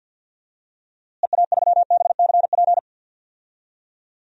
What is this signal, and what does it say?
Morse code, heard as a keyed tone, about 46 wpm, 700 Hz. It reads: EA3BCP
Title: DXCC-100 Call Signs - 6 Characters - Repeated 3x - 50wpm